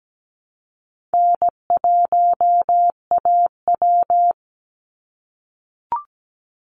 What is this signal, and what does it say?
Morse code, heard as a keyed tone, about 17 wpm, 700 Hz. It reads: N1AW
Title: NA SOTA Chaser Callsigns 17wpm